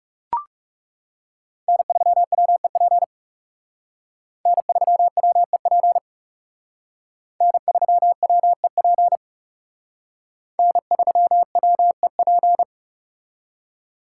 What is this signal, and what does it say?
Morse code, heard as a keyed tone, about 37 wpm, 700 Hz. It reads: N3WEP N3WEP N3WEP N3WEP
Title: Speed Racing - USA Call Signs 30wpm